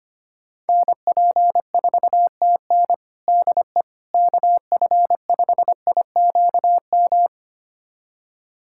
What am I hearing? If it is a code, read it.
Morse code, heard as a keyed tone, about 25 wpm, 700 Hz. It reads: NP4TN DE KF5IQM
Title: USA Callsign DE Callsign 25wpm